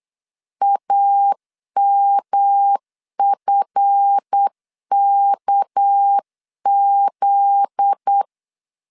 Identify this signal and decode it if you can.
Morse code, heard as a keyed tone, about 8 wpm, 800 Hz. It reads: AMFKZ
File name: LEZIONI / 7 GF9Z / 7 GF9Z 8wpm 3213